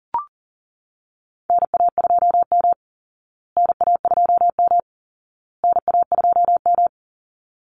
Morse code, heard as a keyed tone, about 40 wpm, 700 Hz. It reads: NA2M NA2M NA2M
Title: Call Signs Repeated 2x - 40wpm